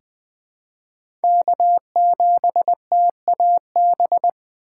Morse code, heard as a keyed tone, about 20 wpm, 700 Hz. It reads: K7TAB
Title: 2023 SOTA Chaser Call Signs - 20wpm